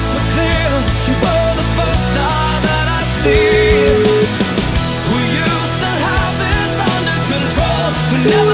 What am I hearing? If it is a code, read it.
Ringback tone (Europe, 425 Hz), with noise.